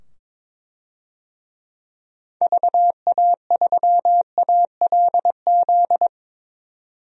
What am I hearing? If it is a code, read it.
Morse code, heard as a keyed tone, about 22 wpm, 700 Hz. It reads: VA3ALZ
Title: DXCC-100 Call Signs - 6 Characters - 22wpm